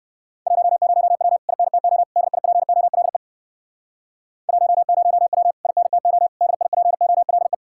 Morse code, heard as a keyed tone, about 59 wpm, 700 Hz. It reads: 10W INTO DIPOLE 10W INTO DIPOLE